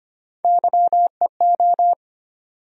Morse code, heard as a keyed tone, about 25 wpm, 700 Hz. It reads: YEO
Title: USA Call Sign Postfixes - 3 Letters 25wpm